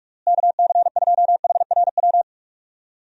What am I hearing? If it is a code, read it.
Morse code, heard as a keyed tone, about 46 wpm, 700 Hz. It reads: KC2HRW
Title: USA Call Signs - 6 Characters - Repeated 3x - 45wpm